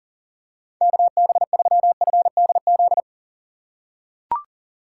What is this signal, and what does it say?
Morse code, heard as a keyed tone, about 37 wpm, 700 Hz. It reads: KB3FDZ
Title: USA Call Signs - 6 Characters - 2x Character Spacing - 40wpm